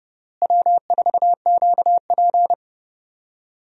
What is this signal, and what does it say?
Morse code, heard as a keyed tone, about 30 wpm, 700 Hz. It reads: W4QP
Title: USA Call Signs - 4 Characters - Repeated 3x - 30wpm